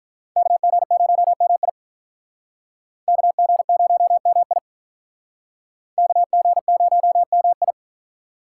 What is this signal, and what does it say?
Morse code, heard as a keyed tone, about 46 wpm, 700 Hz. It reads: KG0MI KG0MI KG0MI